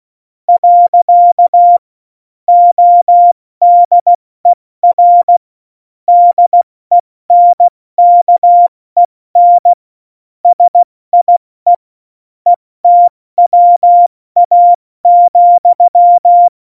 Morse code, heard as a keyed tone, about 16 wpm, 700 Hz. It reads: . ODER DENKEN SIE ETWA,
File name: Morsekurs / Witze / Witz 05 Mai, 2020 70 F80